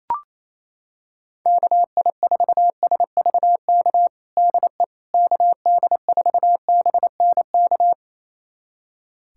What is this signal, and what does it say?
Morse code, heard as a keyed tone, about 28 wpm, 700 Hz. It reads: KI4SVK DE KD4BNK
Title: USA Callsign DE Callsign - 6 Characters - 28wpm